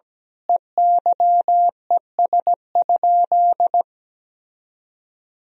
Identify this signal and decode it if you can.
Morse code, heard as a keyed tone, about 17 wpm, 700 Hz. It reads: EYES?
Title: Sentences from Top 1000 Words - Review Part 4 - 17wpm